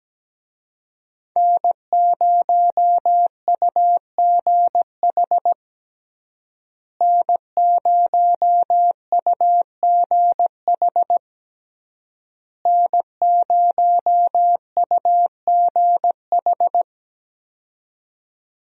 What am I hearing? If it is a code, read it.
Morse code, heard as a keyed tone, about 17 wpm, 700 Hz. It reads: N0UGH N0UGH N0UGH
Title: USA Call Signs - 5 Characters - Repeated 3x - 17wpm